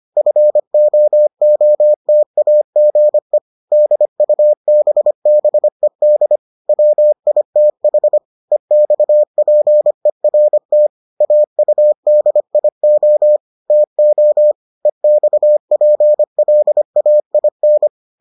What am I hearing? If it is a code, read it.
Morse code, heard as a keyed tone, about 25 wpm, 600 Hz. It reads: FOOTAGE DUBBED WITH EXPERT AUDIO TO EXPLAIN